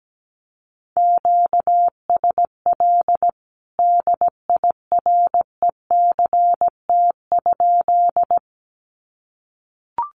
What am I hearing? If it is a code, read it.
Morse code, heard as a keyed tone, about 17 wpm, 700 Hz. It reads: QSL DIRECT?
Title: Common QSO Phrases 17wpm